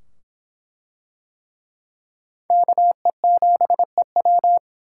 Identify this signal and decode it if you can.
Morse code, heard as a keyed tone, about 23 wpm, 700 Hz. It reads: KE7EW